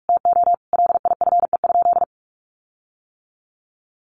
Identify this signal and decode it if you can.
Morse code, heard as a keyed tone, about 44 wpm, 700 Hz. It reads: TO LIFE?
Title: Sentences from Top 1300 Words 45wpm